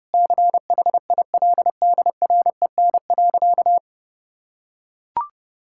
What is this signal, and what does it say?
Morse code, heard as a keyed tone, about 30 wpm, 700 Hz. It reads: CHILDREN.
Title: Sentences from Top 1600 Words 30wpm